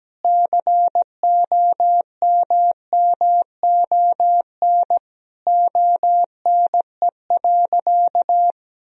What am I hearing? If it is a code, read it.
Morse code, heard as a keyed tone, about 17 wpm, 700 Hz. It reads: COMMON ONE.
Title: Sentences from Top 1000 Words - Review Part 4 - 17wpm